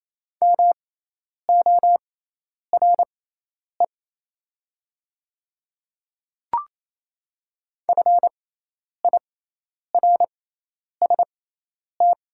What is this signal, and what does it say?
Morse code, heard as a keyed tone, about 28 wpm, spaced as for 10 wpm, 700 Hz. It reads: MORE FIRST
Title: 6x Character Spacing - Top 100 Words - 28wpm